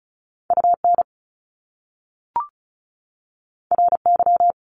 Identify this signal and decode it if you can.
Morse code, heard as a keyed tone, about 36 wpm, 700 Hz. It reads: UN RY
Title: Top Two Letter Combinations in Words 35wpm